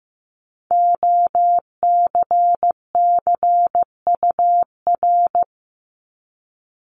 Morse code, heard as a keyed tone, about 15 wpm, 700 Hz. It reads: OCCUR
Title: Multi-letter Phonemes 15wpm